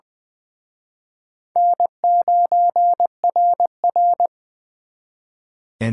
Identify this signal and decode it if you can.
Morse code, heard as a keyed tone, about 24 wpm, 700 Hz. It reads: N9RR
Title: Speed Racing - USA 1x2 Call Signs 20wpm